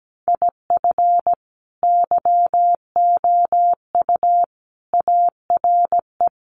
Morse code, heard as a keyed tone, about 17 wpm, 700 Hz. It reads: IF YOU ARE
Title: Sets of 5 Words Encore2 17wpm